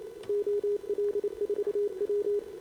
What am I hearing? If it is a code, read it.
Morse code, heard as a keyed tone, about 28 wpm, 410 Hz. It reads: OL4W
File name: CQWWCW2025 / Sobota R 15 40 20m